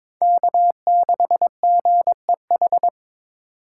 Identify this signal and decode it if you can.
Morse code, heard as a keyed tone, about 22 wpm, 700 Hz. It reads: K6GEH
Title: USA Call Signs - 5 Characters - 4x Character Spacing - 22wpm